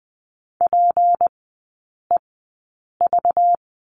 Morse code, heard as a keyed tone, about 20 wpm, spaced as for 8 wpm, 700 Hz. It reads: PEV
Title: 191 Lesson 31e - Letter v - Three Characters 20wpm